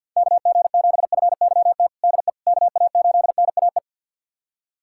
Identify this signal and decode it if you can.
Morse code, heard as a keyed tone, about 51 wpm, 700 Hz. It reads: KG7FYT DE KA8NAE
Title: USA Callsign DE Callsign - 6 Characters - 50wpm